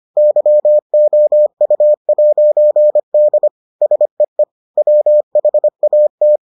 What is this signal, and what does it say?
Morse code, heard as a keyed tone, about 25 wpm, 600 Hz. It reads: YOU'D SEE WHAT